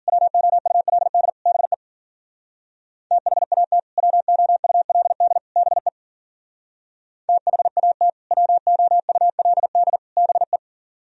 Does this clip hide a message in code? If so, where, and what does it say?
Yes: Morse code from 0.1 to 10.6 s, heard as a keyed tone, about 52 wpm, 700 Hz. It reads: WOULD BE THAT WOULD BE THAT WOULD BE